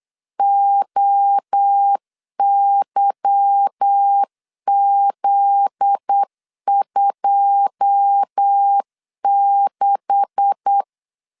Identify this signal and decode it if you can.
Morse code, heard as a keyed tone, about 8 wpm, 800 Hz. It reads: OYZ26
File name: LEZIONI / 9 2867 8wpm 3440